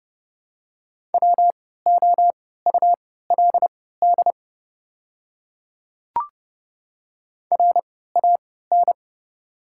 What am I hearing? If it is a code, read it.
Morse code, heard as a keyed tone, about 30 wpm, spaced as for 17 wpm, 700 Hz. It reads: WOULD RAN
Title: 3x Character Spacing - Top 500 Words - 30wpm